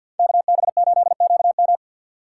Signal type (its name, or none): Morse code